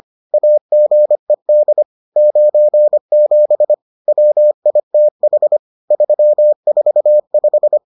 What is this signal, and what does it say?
Morse code, heard as a keyed tone, about 25 wpm, 600 Hz. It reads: AGED 97 WITH 345